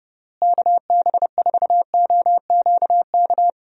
Morse code, heard as a keyed tone, about 32 wpm, 700 Hz. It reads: KB4OQK
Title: USA Call Signs - 6 Characters - Repeated 3x - 30wpm